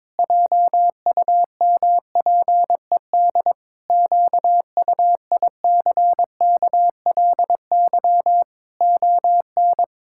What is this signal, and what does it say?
Morse code, heard as a keyed tone, about 22 wpm, 700 Hz. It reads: JUMPED QUICKLY ON
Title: Pangram Challenge v2 - 22wpm